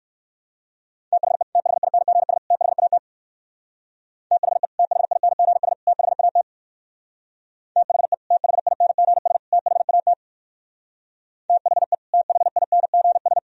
Morse code, heard as a keyed tone, about 64 wpm, 700 Hz. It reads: THE THINGS THAT THE THINGS THAT THE THINGS THAT THE THINGS